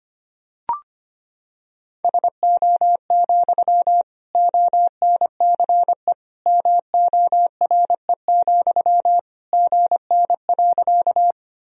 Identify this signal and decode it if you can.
Morse code, heard as a keyed tone, about 25 wpm, 700 Hz. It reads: SO, ONCE MORE, GN.
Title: Sentences from Top 300 Words and Abbreviations 25wpm